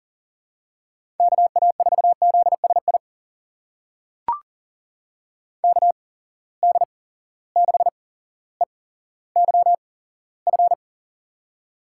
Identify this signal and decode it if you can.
Morse code, heard as a keyed tone, about 40 wpm, spaced as for 17 wpm, 700 Hz. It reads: KA4ZSI KD6EYF
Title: USA Call Signs - 6 Characters - 8x Character Spacing - 40wpm